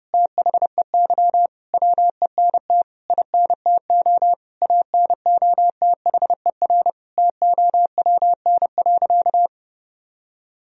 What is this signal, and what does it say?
Morse code, heard as a keyed tone, about 30 wpm, 700 Hz. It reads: THEY WENT INTO ANOTHER TOWN.
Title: Sentences from Top 1000 Words - Review Part 2 - 30wpm